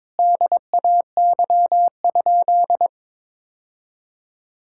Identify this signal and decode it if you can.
Morse code, heard as a keyed tone, about 23 wpm, 700 Hz. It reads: DAY?
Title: Sentences from Top 1000 Words - Review Part 5 - 22wpm